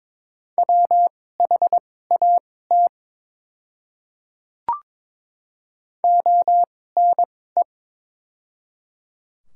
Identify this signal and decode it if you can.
Morse code, heard as a keyed tone, about 22 wpm, spaced as for 16 wpm, 700 Hz. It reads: WHAT ONE
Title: 2x Character Spacing - Top 100 Words - 22wpm